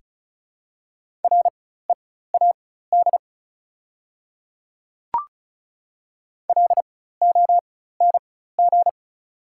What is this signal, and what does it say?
Morse code, heard as a keyed tone, about 36 wpm, spaced as for 16 wpm, 690 Hz. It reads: READ LONG
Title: 4x Character Spacing - Top 200 Words - 35wpm